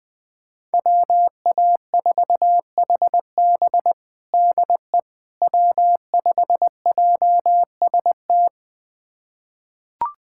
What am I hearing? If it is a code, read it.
Morse code, heard as a keyed tone, about 21 wpm, 700 Hz. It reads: WA4HB DE W5JST
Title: USA Callsign DE Callsign - 5 Characters - 20wpm